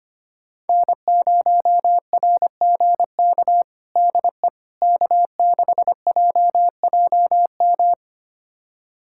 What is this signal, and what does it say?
Morse code, heard as a keyed tone, about 25 wpm, 700 Hz. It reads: N0RGK DE K6JJM
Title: USA Callsign DE Callsign - 5 Characters - 25wpm